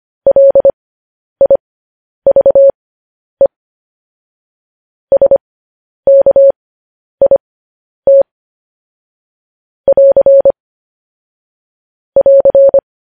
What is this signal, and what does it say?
Morse code, heard as a keyed tone, about 25 wpm, spaced as for 10 wpm, 550 Hz. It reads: LIVE SKIT <AR> <AR>
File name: podcast / News headlines-1x-25x10